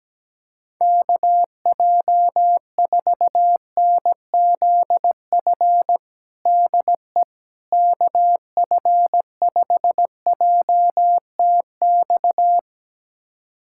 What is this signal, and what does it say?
Morse code, heard as a keyed tone, about 17 wpm, 700 Hz. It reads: KJ4NZF DE KF5JTX
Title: USA Callsign DE Callsign - 6 Characters - 17wpm